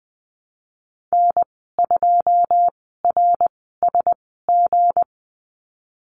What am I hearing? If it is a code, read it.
Morse code, heard as a keyed tone, about 20 wpm, 700 Hz. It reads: N 2 R S G
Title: 200 Lesson 32g - Letter k - Call Signs 20wpm